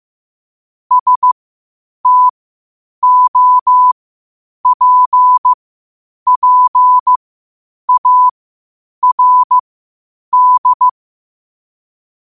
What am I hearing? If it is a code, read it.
Morse code, heard as a keyed tone, about 15 wpm, spaced as for 8 wpm, 1000 Hz. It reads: STOPPARD
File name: qotdfiles / 8wpm